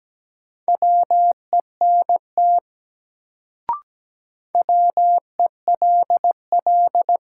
Morse code, heard as a keyed tone, about 17 wpm, 700 Hz. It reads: WENT WELL
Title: Sentences from Top 200 Words - New Word Drill - 17wpm